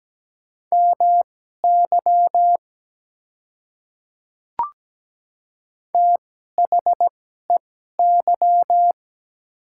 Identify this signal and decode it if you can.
Morse code, heard as a keyed tone, about 17 wpm, spaced as for 12 wpm, 700 Hz. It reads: MY THEY
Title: 2x Character Spacing - Top 100 Words - 17wpm